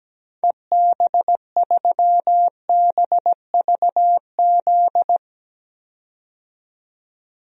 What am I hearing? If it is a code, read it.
Morse code, heard as a keyed tone, about 17 wpm, 700 Hz. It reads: EB3BVZ
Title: DXCC-100 Call Signs - 6 Characters - 17wpm